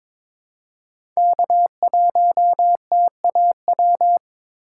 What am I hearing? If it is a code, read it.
Morse code, heard as a keyed tone, about 23 wpm, 700 Hz. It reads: K1TAW
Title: USA Call Signs - 5 Characters - 8x Character Spacing - 22wpm